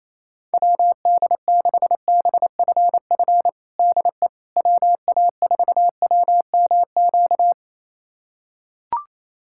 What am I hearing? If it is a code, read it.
Morse code, heard as a keyed tone, about 28 wpm, 700 Hz. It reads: WD6BFF DE WA4WMQ
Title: USA Callsign DE Callsign - 6 Characters - 28wpm